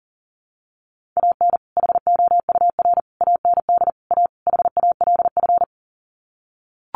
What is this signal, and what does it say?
Morse code, heard as a keyed tone, about 40 wpm, 700 Hz. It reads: AN HOUR AND A HALF
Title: Sets of 5 Words 40wpm